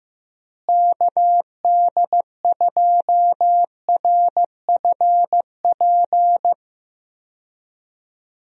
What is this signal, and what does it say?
Morse code, heard as a keyed tone, about 15 wpm, 700 Hz. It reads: KD2RFP
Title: USA Call Signs - 6 Characters - Repeated 3x - 15wpm